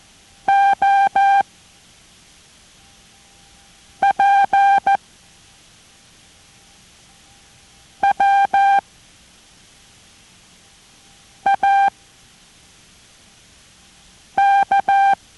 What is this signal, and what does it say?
Morse code, heard as a keyed tone, about 14 wpm, 790 Hz. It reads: O P W A K